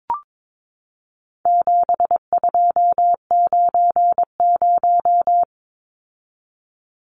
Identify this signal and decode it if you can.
Morse code, heard as a keyed tone, about 22 wpm, 700 Hz. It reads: 7290
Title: Numbers - 4-Digit 22wpm